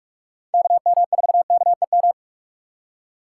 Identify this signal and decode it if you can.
Morse code, heard as a keyed tone, about 46 wpm, 700 Hz. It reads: KM4KEM
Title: USA Call Signs - 6 Characters - 6x Character Spacing - 45wpm